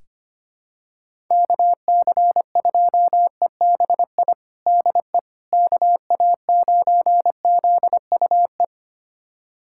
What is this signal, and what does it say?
Morse code, heard as a keyed tone, about 25 wpm, 700 Hz. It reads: KC2EBI DE KA9ZUE
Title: USA Callsign DE Callsign 25wpm